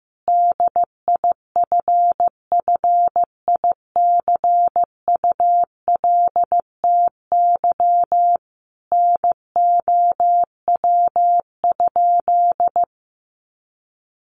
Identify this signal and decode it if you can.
Morse code, heard as a keyed tone, about 15 wpm, 700 Hz. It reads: DIFFICULTY NOW?
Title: Sentences from Top 1500 Words 15wpm